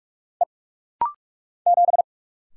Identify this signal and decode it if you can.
Morse code, heard as a keyed tone, about 45 wpm, 700 Hz. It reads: E 7
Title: Single Letter-Number Alt - Rapid-Fire 45wpm